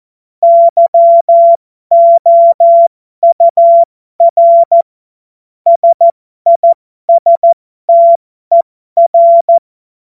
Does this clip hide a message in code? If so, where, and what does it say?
Yes: Morse code from 0.4 to 9.6 s, heard as a keyed tone, about 14 wpm, spaced as for 12 wpm, 680 Hz. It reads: YOUR SISTER